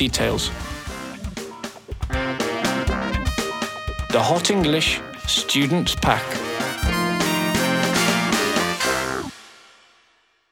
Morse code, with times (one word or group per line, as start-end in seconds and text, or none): none